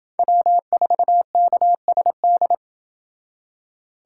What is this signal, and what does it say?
Morse code, heard as a keyed tone, about 29 wpm, 700 Hz. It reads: W4KSD